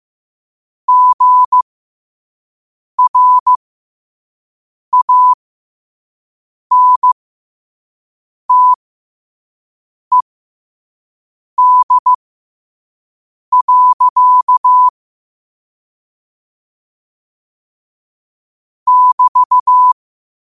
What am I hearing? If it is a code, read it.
Morse code, heard as a keyed tone, about 15 wpm, spaced as for 5 wpm, 1000 Hz. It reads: GRANTED. <BT>
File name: qotdfiles / 5wpm